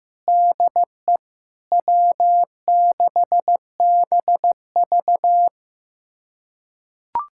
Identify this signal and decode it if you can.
Morse code, heard as a keyed tone, about 15 wpm, 700 Hz. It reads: DE W6BV
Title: USA Callsign DE Callsign - 4 Characters - 15wpm